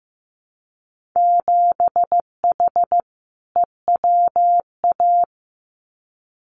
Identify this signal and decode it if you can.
Morse code, heard as a keyed tone, about 15 wpm, 700 Hz. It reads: 7H EWA
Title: Winter Field Day Exchanges Encore 15wpm